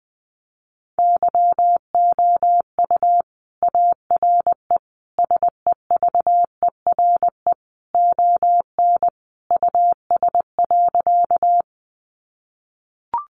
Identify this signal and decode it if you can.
Morse code, heard as a keyed tone, about 20 wpm, 700 Hz. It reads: YOU ARE SEVERE ON US.
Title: Sentences from Top 1400 Words 20wpm